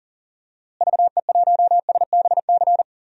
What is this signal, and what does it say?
Morse code, heard as a keyed tone, about 40 wpm, 700 Hz. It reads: VE1SBC
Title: DXCC-100 Call Signs - 6 Characters - Repeated 2x - 40wpm